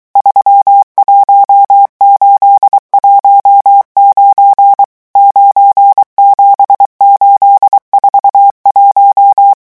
Morse code, heard as a keyed tone, about 23 wpm, 800 Hz. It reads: 31819 97841
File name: morse mp3 / revision11 chiffres vitesse 22